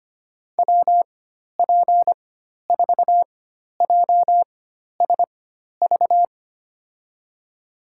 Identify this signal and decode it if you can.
Morse code, heard as a keyed tone, about 25 wpm, spaced as for 12 wpm, 700 Hz. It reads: WP4JSV
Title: USA Call Signs - 6 Characters - 4x Character Spacing - 25wpm